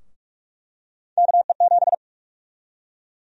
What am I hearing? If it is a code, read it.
Morse code, heard as a keyed tone, about 47 wpm, 700 Hz. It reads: KE7
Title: USA Call Sign Prefixes - Repeated 2x - 45wpm